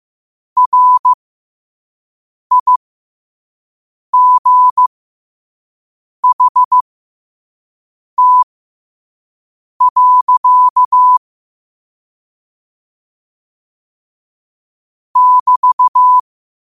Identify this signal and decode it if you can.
Morse code, heard as a keyed tone, about 15 wpm, spaced as for 5 wpm, 1000 Hz. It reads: RIGHT. <BT>
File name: qotdfiles / 5wpm